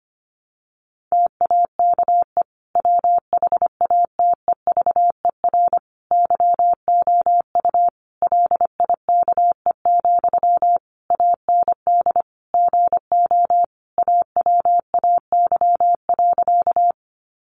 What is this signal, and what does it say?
Morse code, heard as a keyed tone, about 25 wpm, 700 Hz. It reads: TAKE WHATEVER YOU LIKE, AND GO AWAY.
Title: Sentences from Top 1100 Words 25wpm